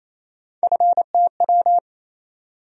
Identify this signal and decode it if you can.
Morse code, heard as a keyed tone, about 28 wpm, 700 Hz. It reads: FTW
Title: USA Call Sign Postfixes - 3 Letters 28wpm